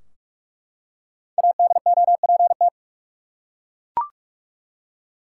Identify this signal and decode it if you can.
Morse code, heard as a keyed tone, about 42 wpm, 700 Hz. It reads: ADOPT
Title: Five Letter Words - 45wpm